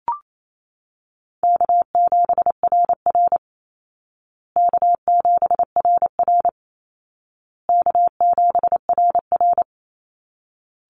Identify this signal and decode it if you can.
Morse code, heard as a keyed tone, about 28 wpm, 700 Hz. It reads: K7RR K7RR K7RR
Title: Call Signs Repeated 2x - 28wpm